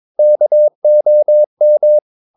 Morse code, heard as a keyed tone, about 22 wpm, 600 Hz. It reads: KOM
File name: Ferie QRS 22wpm